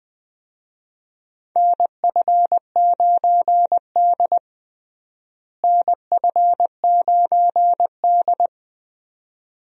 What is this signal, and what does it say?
Morse code, heard as a keyed tone, about 20 wpm, 700 Hz. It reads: NF9D NF9D
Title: USA Call Signs - 4 Characters - Repeated 2x - 20wpm